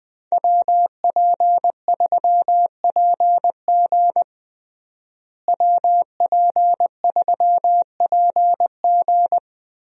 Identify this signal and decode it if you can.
Morse code, heard as a keyed tone, about 21 wpm, 700 Hz. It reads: WP3PG WP3PG
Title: USA Call Signs - 5 Characters - Repeated 3x - 20wpm